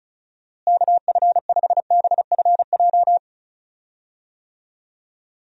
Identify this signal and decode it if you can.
Morse code, heard as a keyed tone, about 36 wpm, 700 Hz. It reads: KF5BFJ
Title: USA Call Signs - 6 Characters - Repeated 3x - 35wpm